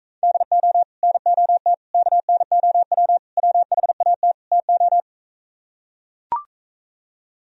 Morse code, heard as a keyed tone, about 43 wpm, 700 Hz. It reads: DO NOT KNOW WHAT TO